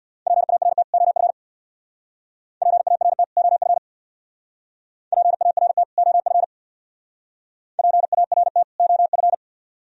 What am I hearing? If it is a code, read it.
Morse code, heard as a keyed tone, about 59 wpm, 700 Hz. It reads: PART OF PART OF PART OF PART OF
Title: Speed-Racing - Sets of 2 Words 50wpm